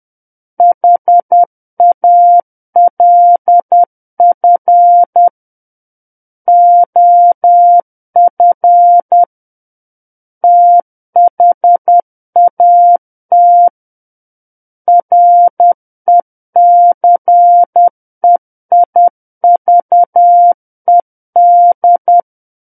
Morse code, heard as a keyed tone, about 10 wpm, 700 Hz. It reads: HALF OF THAT RECEIVED
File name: Book One / 01-Chapter 1 - 10 WPM